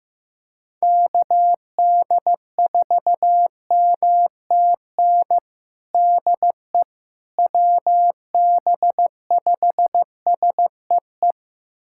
Morse code, heard as a keyed tone, about 15 wpm, 700 Hz. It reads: KD4MTN DE WB5SEE
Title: USA Callsign DE Callsign - 6 Characters - 15wpm